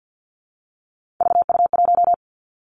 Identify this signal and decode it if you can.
Morse code, heard as a keyed tone, about 50 wpm, 700 Hz. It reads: VU1
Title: Three Challenging Letter-Number Combinations 50wpm